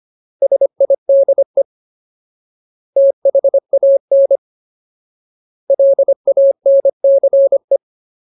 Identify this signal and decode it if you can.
Morse code, heard as a keyed tone, about 25 wpm, 550 Hz. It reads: SIDE THAN LANCE
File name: Ultimate Word List / 25WPM Ultimate